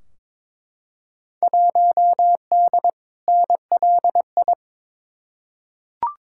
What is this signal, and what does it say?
Morse code, heard as a keyed tone, about 21 wpm, 700 Hz. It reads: 1D NLI
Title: Field Day Exchanges - 22wpm